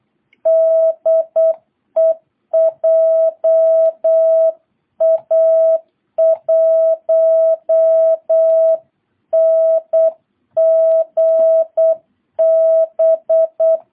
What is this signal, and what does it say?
Morse code, heard as a keyed tone, about 8 wpm, 660 Hz. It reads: DE JA1NGB